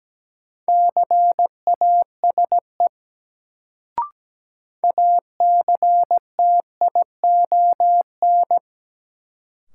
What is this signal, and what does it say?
Morse code, heard as a keyed tone, about 17 wpm, 700 Hz. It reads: CASE ACTION
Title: Sentences from Top 700 Words - New Word Drill - 17wpm